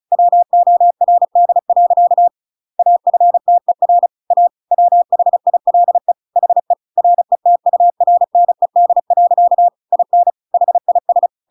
Morse code, heard as a keyed tone, about 34 wpm, 700 Hz. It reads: WORD. AFTER A WHILE HE RETURNED. IN HIS
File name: Pinocchio / Pinocchio ch8 35 35 ik1hge